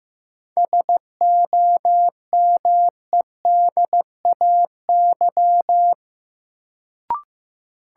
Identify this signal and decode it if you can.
Morse code, heard as a keyed tone, about 15 wpm, 700 Hz. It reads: SOMEDAY
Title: Sentences from Top 1600 Words - New Word Drill - 15wpm